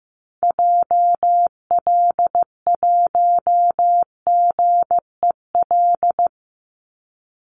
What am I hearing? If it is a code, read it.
Morse code, heard as a keyed tone, about 15 wpm, 700 Hz. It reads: JL1GEL
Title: Call Signs Repeated 3x - 15wpm